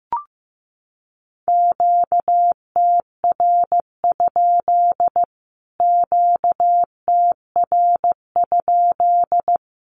Morse code, heard as a keyed tone, about 15 wpm, 700 Hz. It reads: QTR? QTR?
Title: Q-Codes Repeated 3x 15wpm